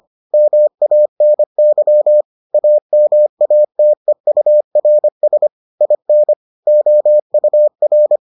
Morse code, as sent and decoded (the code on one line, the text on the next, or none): -- .- -. -.-- / .- -- .- - . ..- .-. ... / .. -. / --- ..- .-.
MANY AMATEURS IN OUR